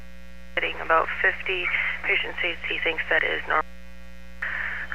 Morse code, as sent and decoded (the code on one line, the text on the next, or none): none